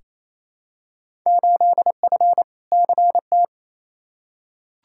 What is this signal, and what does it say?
Morse code, heard as a keyed tone, about 28 wpm, 700 Hz. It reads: 8F CT
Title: Field Day Exchanges Encore - 28wpm